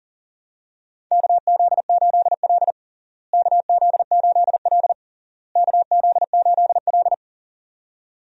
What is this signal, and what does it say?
Morse code, heard as a keyed tone, about 37 wpm, 700 Hz. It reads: KZ8L KZ8L KZ8L
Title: USA Call Signs - 4 Characters - Repeated 3x - 40wpm